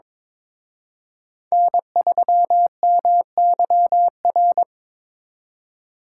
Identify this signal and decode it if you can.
Morse code, heard as a keyed tone, about 25 wpm, 700 Hz. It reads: N3MYR
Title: Speed Racing - USA 1x3 Call Signs 22wpm